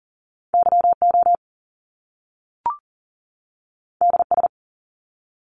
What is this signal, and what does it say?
Morse code, heard as a keyed tone, about 40 wpm, 700 Hz. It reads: YO DS
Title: Top Two Letter Combinations in Words 40wpm